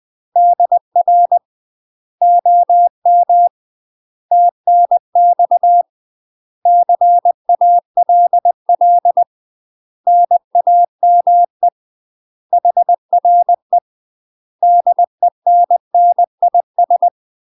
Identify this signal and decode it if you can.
Morse code, heard as a keyed tone, about 20 wpm, 700 Hz. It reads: DR OM TNX CALL NAME HRE DENNIS
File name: UK MASTER QSOs TRAINING / 20WPM / UK MASTER QSOs 0001